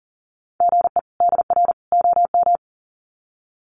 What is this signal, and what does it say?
Morse code, heard as a keyed tone, about 40 wpm, 700 Hz. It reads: GE DR OM
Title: Common QSO Phrases 40wpm